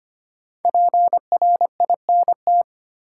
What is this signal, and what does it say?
Morse code, heard as a keyed tone, about 25 wpm, 700 Hz. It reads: PRINT
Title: Sentences from Top 1400 Words - New Word Drill Alt - 25wpm